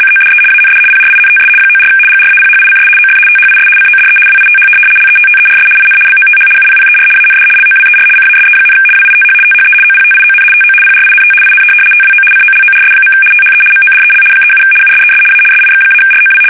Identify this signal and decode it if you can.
RTTY, about 75 baud. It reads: CODE AND PRINTS 94 GRAPHICS OF THIS CODE WITH LETTERS IN BOTH UPPER AND LOWER CASE, WITH THE OPTION TO PRINT IN TWO COLORS. THE MODEL 37 USES A SIX-ROW
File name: fsk850-75B / teletype-corp